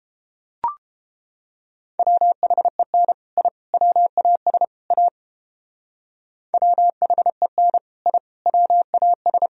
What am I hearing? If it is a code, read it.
Morse code, heard as a keyed tone, about 31 wpm, 700 Hz. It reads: WHEN I WAS A WHEN I WAS